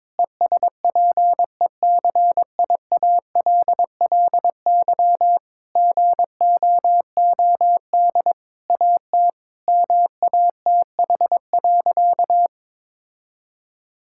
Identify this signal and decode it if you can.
Morse code, heard as a keyed tone, about 22 wpm, 700 Hz. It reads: ESPECIALLY GOOD AT MATH.
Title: Sentences from Top 1000 Words - Review Part 6 - 22wpm